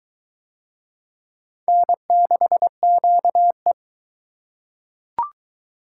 Morse code, heard as a keyed tone, about 20 wpm, 700 Hz. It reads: N6QE